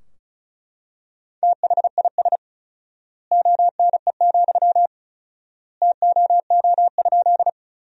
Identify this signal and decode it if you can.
Morse code, heard as a keyed tone, about 36 wpm, 700 Hz. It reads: THIS ONE, TOO?
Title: 4x Word-Spacing - Sentences from Top 200 Words - 35wpm